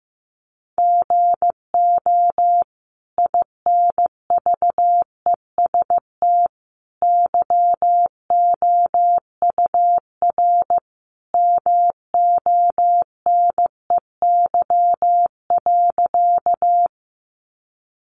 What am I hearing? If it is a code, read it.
Morse code, heard as a keyed tone, about 15 wpm, 700 Hz. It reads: GO INVEST YOUR MONEY.
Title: Sentences from Top 1100 Words 15wpm